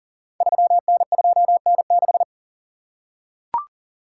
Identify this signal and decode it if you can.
Morse code, heard as a keyed tone, about 42 wpm, 690 Hz. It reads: 3N2N6
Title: Code Groups - 5 Letter:Numbers - 40wpm